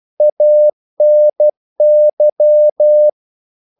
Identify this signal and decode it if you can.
Morse code, heard as a keyed tone, about 12 wpm, 600 Hz. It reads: ANY
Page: English quotations in CW (telegraphy - morse)